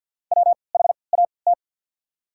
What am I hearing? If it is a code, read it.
Morse code, heard as a keyed tone, about 48 wpm, spaced as for 28 wpm, 700 Hz. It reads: WHAT
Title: 3x Character Spacing - Top 100 Words - 50wpm